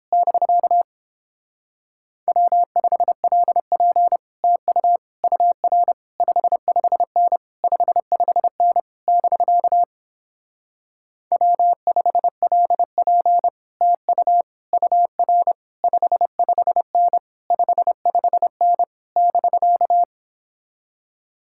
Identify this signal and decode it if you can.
Morse code, heard as a keyed tone, about 29 wpm, 700 Hz. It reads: <BK> W5LP TU UR 55N 55N <BK> W5LP TU UR 55N 55N <BK>